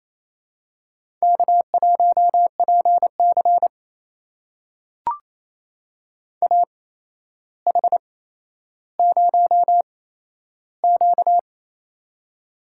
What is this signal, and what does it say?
Morse code, heard as a keyed tone, about 28 wpm, spaced as for 12 wpm, 700 Hz. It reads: K1PC AH0Q
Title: USA Call Signs - 8x Character Spacing - 28wpm